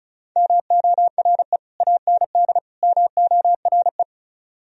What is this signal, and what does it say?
Morse code, heard as a keyed tone, about 37 wpm, 700 Hz. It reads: MORE AND MORE
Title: Speed-Racing - Sets of 3 Words 30wpm